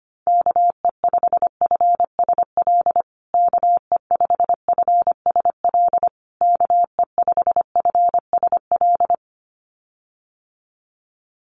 Morse code, heard as a keyed tone, about 25 wpm, 700 Hz. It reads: KE5FSL KE5FSL KE5FSL
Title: USA Call Signs Repeated 3x 25wpm